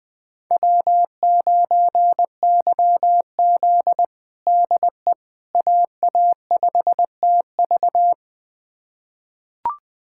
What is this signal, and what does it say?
Morse code, heard as a keyed tone, about 20 wpm, 700 Hz. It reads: W9YZ DE AA5TV
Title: USA Callsign DE Callsign 20wpm